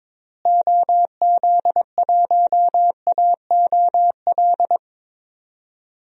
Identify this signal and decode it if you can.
Morse code, heard as a keyed tone, about 22 wpm, 700 Hz. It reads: OZ1AOL
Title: DXCC-100 Call Signs - 6 Characters - Repeated 2x - 22wpm